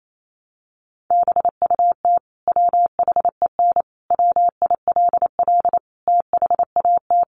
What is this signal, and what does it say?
Morse code, heard as a keyed tone, about 28 wpm, 700 Hz. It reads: BUT WHEN WILL THAT
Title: Sentences from Top 100 Words 28wpm